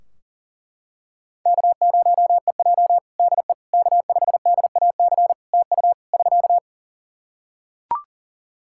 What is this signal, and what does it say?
Morse code, heard as a keyed tone, about 40 wpm, 700 Hz. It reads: K0EJ DE K5DAC TU <SK>
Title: Practice QSO - Exchange 4 - Ending - 40wpm